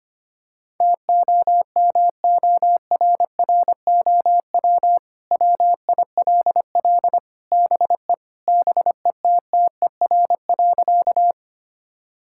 Morse code, heard as a keyed tone, about 25 wpm, 700 Hz. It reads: TOMORROW WILL BE BETTER.
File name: practice / 25wpm / Sentences with 2-Word Phrases - 25wpm